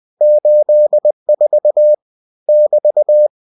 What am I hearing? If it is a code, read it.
Morse code, heard as a keyed tone, about 20 wpm, 600 Hz. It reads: 84 <BT>
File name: cw-books / winnie-the-pooh / winnie-20wpm-page-0084